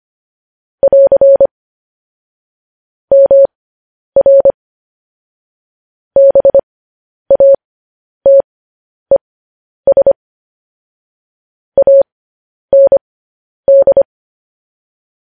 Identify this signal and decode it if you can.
Morse code, heard as a keyed tone, about 25 wpm, spaced as for 10 wpm, 550 Hz. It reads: <AR> MR BATES AND
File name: podcast / News headlines-1x-25x10